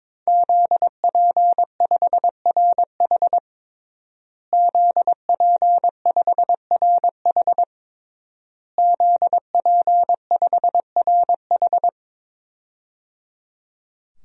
Morse code, heard as a keyed tone, about 22 wpm, 700 Hz. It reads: ZP5RH ZP5RH ZP5RH
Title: DXCC-100 Call Signs - 5 Characters - Repeated 3x - 22wpm